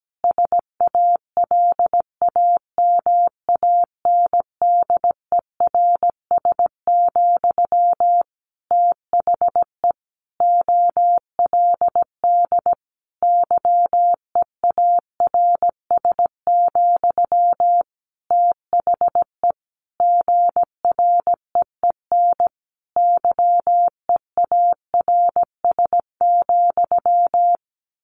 Morse code, heard as a keyed tone, about 17 wpm, 700 Hz. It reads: SALAMANDERS, THE OLD YEARS, THE GREEN YEARS,
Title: A Sound Of Thunder 17wpm